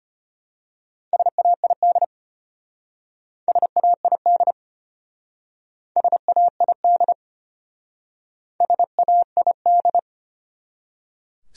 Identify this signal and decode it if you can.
Morse code, heard as a keyed tone, about 31 wpm, 700 Hz. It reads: SAID SAID SAID SAID
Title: Speed-Racing Top 100 Words 25wpm